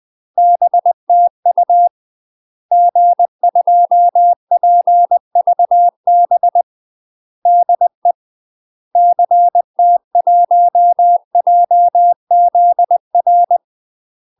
Morse code, heard as a keyed tone, about 20 wpm, 700 Hz. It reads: BTU G2PVB DE CT1JZR
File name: UK MASTER QSOs TRAINING / 20WPM / UK MASTER QSOs 0038